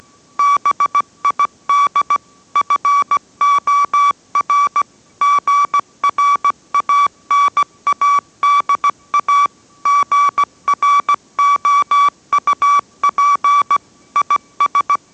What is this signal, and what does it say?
Morse code, heard as a keyed tone, about 17 wpm, 1150 Hz. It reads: BID FOR GRANADA GROUP IS